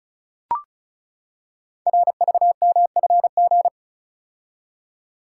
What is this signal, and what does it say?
Morse code, heard as a keyed tone, about 36 wpm, 700 Hz. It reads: RVMFG
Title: Code Groups - 5 Letter:Numbers - 35wpm